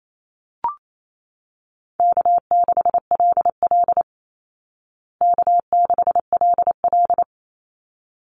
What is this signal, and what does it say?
Morse code, heard as a keyed tone, about 28 wpm, 700 Hz. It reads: K6LL K6LL
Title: Call Signs Repeated 3x - 28wpm